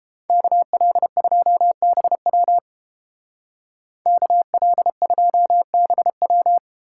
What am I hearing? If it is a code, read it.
Morse code, heard as a keyed tone, about 32 wpm, 700 Hz. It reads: KL2BW KL2BW